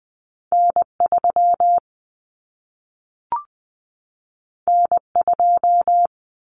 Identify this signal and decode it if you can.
Morse code, heard as a keyed tone, about 20 wpm, 700 Hz. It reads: N3 N2
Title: Common Cut Numbers - 2-Digit 20wpm